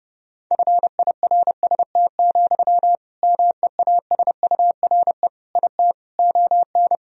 Morse code, heard as a keyed tone, about 30 wpm, 700 Hz. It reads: FIRST, MEASURE IT ON
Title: Sentences from Top 1000 Words - Review Part 3 - 30wpm